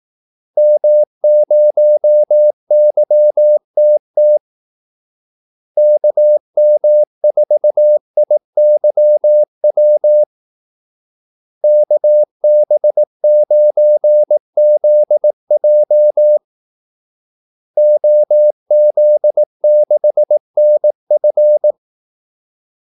Morse code, heard as a keyed tone, about 18 wpm, 600 Hz. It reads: M0YTT KM4IYW KB9ZJ OZ6NF
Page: Kaldesignaler 18 wpm